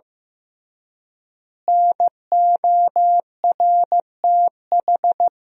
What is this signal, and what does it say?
Morse code, heard as a keyed tone, about 15 wpm, 700 Hz. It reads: NORTH
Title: 3x Word-Spacing - Sentences from Top 400 Words - 15wpm